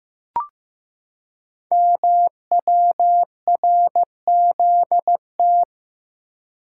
Morse code, heard as a keyed tone, about 15 wpm, 700 Hz. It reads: MWRZT
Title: Code Groups - 5 Letters - 15wpm